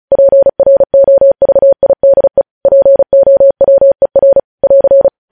Morse code, heard as a keyed tone, about 36 wpm, 550 Hz. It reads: PROVIDE POWER <AR>
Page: news headlines in morse code at 35 wpm